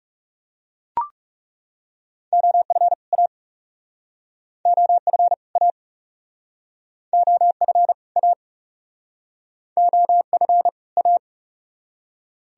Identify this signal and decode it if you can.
Morse code, heard as a keyed tone, about 40 wpm, 700 Hz. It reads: OF A OF A OF A OF A